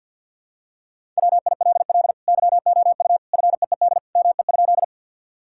Morse code, heard as a keyed tone, about 50 wpm, 700 Hz. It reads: WILL YOU FEED ME?
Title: Sentences from Top 1600 Words 50wpm